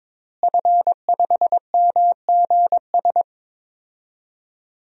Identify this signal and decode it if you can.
Morse code, heard as a keyed tone, about 22 wpm, 700 Hz. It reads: F5MGS
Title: DXCC-100 Call Signs - 5 Characters - Repeated 3x - 22wpm